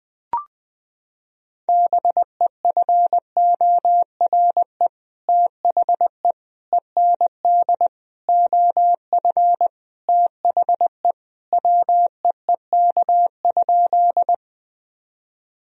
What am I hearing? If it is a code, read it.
Morse code, heard as a keyed tone, about 20 wpm, 700 Hz. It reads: BEFORE THE END OF THE WEEK?
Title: Sentences from Top 1000 Words - Review Part 2 - 20wpm